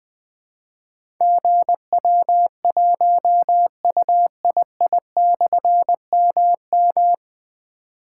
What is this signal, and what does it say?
Morse code, heard as a keyed tone, about 20 wpm, 700 Hz. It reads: GW1UII/MM
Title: Difficult Call Signs 20wpm